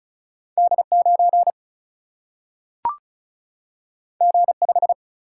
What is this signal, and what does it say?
Morse code, heard as a keyed tone, about 36 wpm, 700 Hz. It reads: D9 G5
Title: Single-Letter Single-Number 35wpm